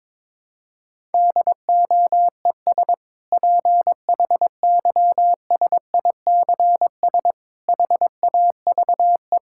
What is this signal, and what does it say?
Morse code, heard as a keyed tone, about 21 wpm, 700 Hz. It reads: DOES PHYSICS HAVE
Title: Sentences from Top 1000 Words - Review Part 5 - 22wpm